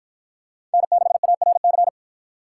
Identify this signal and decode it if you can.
Morse code, heard as a keyed tone, about 51 wpm, 700 Hz. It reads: N6ARC